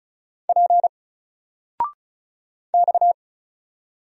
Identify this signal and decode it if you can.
Morse code, heard as a keyed tone, about 36 wpm, 700 Hz. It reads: P X
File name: Single-Letter-Number-35wpm